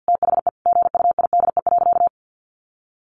Morse code, heard as a keyed tone, about 50 wpm, 700 Hz. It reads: THE GUIDE.
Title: Sentences from Top 600 Words 50wpm